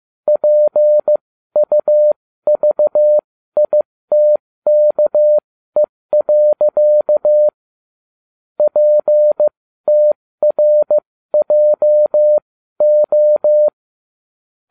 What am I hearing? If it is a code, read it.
Morse code, heard as a keyed tone, about 15 wpm, spaced as for 13 wpm, 600 Hz. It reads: PUVITKE. PTRJO